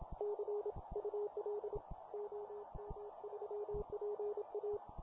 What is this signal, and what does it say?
Morse code, heard as a keyed tone, about 27 wpm, 420 Hz, with radio noise. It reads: CUL OM3PA